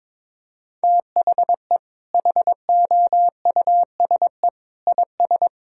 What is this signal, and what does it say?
Morse code, heard as a keyed tone, about 22 wpm, 700 Hz. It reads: THE HOUSE IS
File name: practice / 22wpm / Sentences with 2-Word Phrases - 22wpm